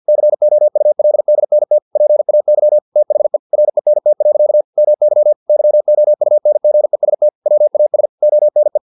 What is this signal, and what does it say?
Morse code, heard as a keyed tone, about 50 wpm, 600 Hz. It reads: COULDNT PAY THE RENT. MY YOUNGEST WAS ONE
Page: Never 49wpm